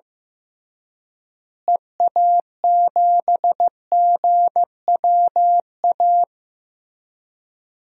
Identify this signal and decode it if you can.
Morse code, heard as a keyed tone, about 15 wpm, 700 Hz. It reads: EA7GWA
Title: DXCC-100 Call Signs - 6 Characters - Repeated 3x - 15wpm